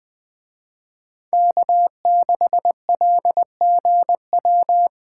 Morse code, heard as a keyed tone, about 20 wpm, 700 Hz. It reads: K6LGW
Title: Speed Racing - USA Call Signs 17wpm